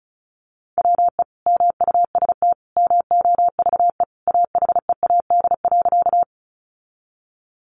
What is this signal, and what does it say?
Morse code, heard as a keyed tone, about 34 wpm, 700 Hz. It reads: WE MUST MOVE AHEAD.
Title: Sentences from Top 1100 Words 35wpm